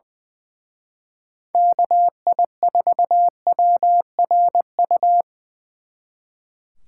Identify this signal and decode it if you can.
Morse code, heard as a keyed tone, about 23 wpm, 700 Hz. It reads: KI4WRU
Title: Speed Racing - USA 2x3 Call Signs 20wpm